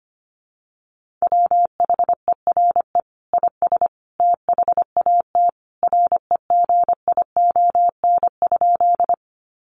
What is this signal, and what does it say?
Morse code, heard as a keyed tone, about 25 wpm, 700 Hz. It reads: WHERE IS THAT REGION?
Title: Sentences from Top 1500 Words 25wpm